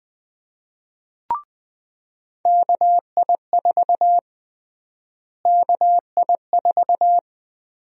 Morse code, heard as a keyed tone, about 20 wpm, 700 Hz. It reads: KI4 KI4
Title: USA Call Sign Prefixes - Repeated 2x - 20wpm